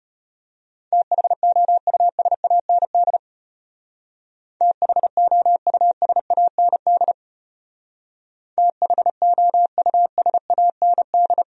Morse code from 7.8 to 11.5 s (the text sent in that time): THOUSAND